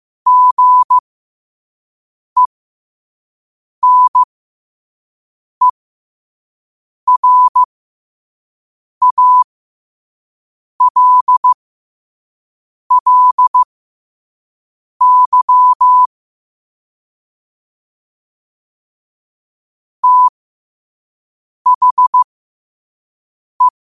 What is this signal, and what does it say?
Morse code, heard as a keyed tone, about 15 wpm, spaced as for 5 wpm, 1000 Hz. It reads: GENERALLY THE
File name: qotdfiles / 5wpm